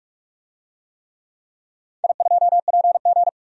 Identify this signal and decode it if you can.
Morse code, heard as a keyed tone, about 46 wpm, 700 Hz. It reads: I2PG
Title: DXCC-100 Call Signs - 4 Characters - 45wpm